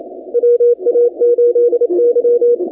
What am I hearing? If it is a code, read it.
Morse code, heard as a keyed tone, about 28 wpm, 480 Hz. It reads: WA8Y
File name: contests / 2019-cw-wpx-kt5j-k5pi- -2330 0-25